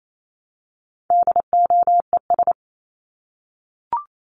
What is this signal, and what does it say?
Morse code, heard as a keyed tone, about 28 wpm, 700 Hz. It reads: DOES
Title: ES:S Suffix - Top 100 Words with Base Words 28wpm